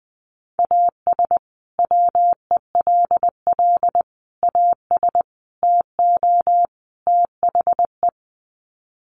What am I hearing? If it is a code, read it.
Morse code, heard as a keyed tone, about 20 wpm, 700 Hz. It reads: AS WELL AS TO THE
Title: Sets of 5 Words Encore2 20wpm